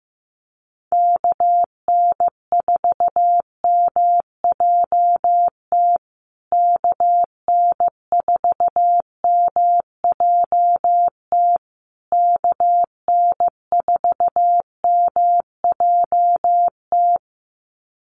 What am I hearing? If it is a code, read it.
Morse code, heard as a keyed tone, about 14 wpm, 700 Hz. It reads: KN4MJT KN4MJT KN4MJT
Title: USA Call Signs Repeated 3x 15wpm